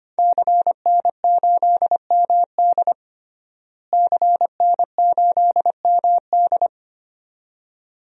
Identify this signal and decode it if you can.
Morse code, heard as a keyed tone, about 25 wpm, 700 Hz. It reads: CN8MD CN8MD
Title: DXCC-100 Call Signs - 5 Characters - Repeated 2x - 25wpm